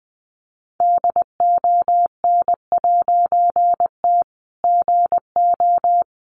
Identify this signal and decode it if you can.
Morse code, heard as a keyed tone, about 20 wpm, 700 Hz. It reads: DON'T GO
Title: A Sound Of Thunder 20wpm